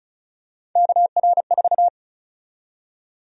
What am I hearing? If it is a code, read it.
Morse code, heard as a keyed tone, about 34 wpm, 700 Hz. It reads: KR4
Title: USA Call Sign Prefixes - 35wpm